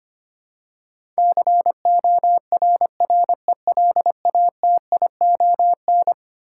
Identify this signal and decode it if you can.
Morse code, heard as a keyed tone, about 25 wpm, 700 Hz. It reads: CORRELATION
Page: morse-code-ninja-practice-25wpm